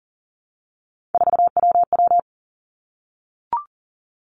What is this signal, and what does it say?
Morse code, heard as a keyed tone, about 38 wpm, 700 Hz. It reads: 4WW
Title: Three Challenging Letter-Number Combinations 40wpm